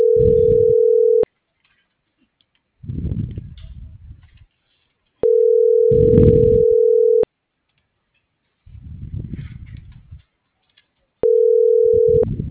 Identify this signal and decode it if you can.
Ringback tone (North America), 440 + 480 Hz.